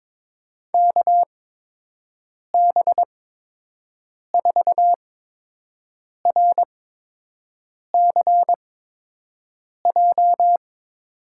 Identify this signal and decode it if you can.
Morse code, heard as a keyed tone, about 22 wpm, spaced as for 6 wpm, 700 Hz. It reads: KB4RCJ
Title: USA Call Signs - 6 Characters - 8x Character Spacing - 22wpm